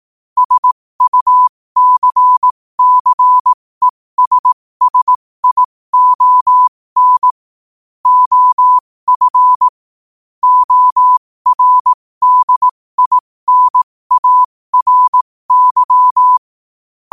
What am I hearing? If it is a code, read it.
Morse code, heard as a keyed tone, about 18 wpm, spaced as for 15 wpm, 1000 Hz. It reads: SUCCESSION OF ORDINARY
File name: qotdfiles / 15wpm